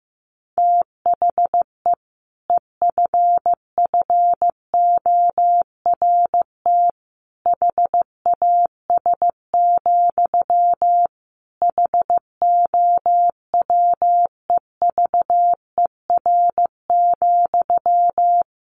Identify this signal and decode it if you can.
Morse code, heard as a keyed tone, about 15 wpm, 700 Hz. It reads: THE EFFORT HAS, HOWEVER,
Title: Sentences from Top 1200 Words 15wpm